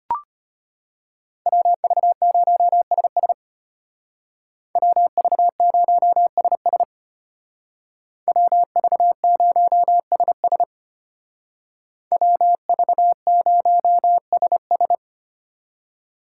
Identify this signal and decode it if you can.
Morse code, heard as a keyed tone, about 33 wpm, 700 Hz. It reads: WV0SS WV0SS WV0SS WV0SS